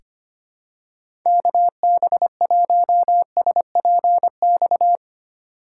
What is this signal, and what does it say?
Morse code, heard as a keyed tone, about 25 wpm, 700 Hz. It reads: KB1SPX
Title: USA Call Signs - 6 Characters - 6x Character Spacing - 25wpm